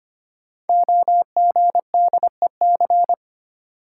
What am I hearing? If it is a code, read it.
Morse code, heard as a keyed tone, about 25 wpm, 700 Hz. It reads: OGDEC
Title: Code Groups - 5 Letters - 25wpm